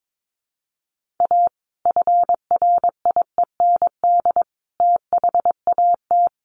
Morse code, heard as a keyed tone, about 22 wpm, 700 Hz. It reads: A FRIEND THAT
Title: Sentences from Top 1500 Words 22wpm